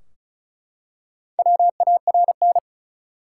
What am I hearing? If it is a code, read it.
Morse code, heard as a keyed tone, about 36 wpm, 700 Hz. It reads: WARN
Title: Four Letter Words - 35wpm